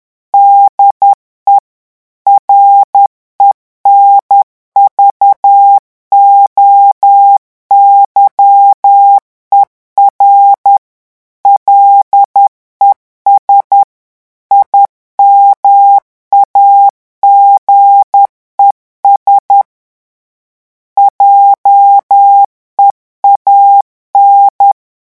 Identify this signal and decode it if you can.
Morse code, heard as a keyed tone, about 10 wpm, 800 Hz. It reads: DE RENVOYER LES IMAGES JEAN
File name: revision9 citations humoristiques vitesse 10 mots minute